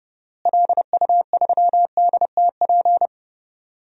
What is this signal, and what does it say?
Morse code, heard as a keyed tone, about 30 wpm, 700 Hz. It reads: LU3DTP
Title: DXCC-100 Call Signs - 6 Characters - Repeated 3x - 30wpm